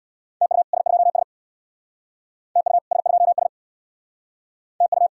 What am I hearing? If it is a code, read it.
Morse code, heard as a keyed tone, about 62 wpm, 700 Hz. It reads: TU S2S TU S2S TU
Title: Speed-Racing Common QSO Phrases - 50wpm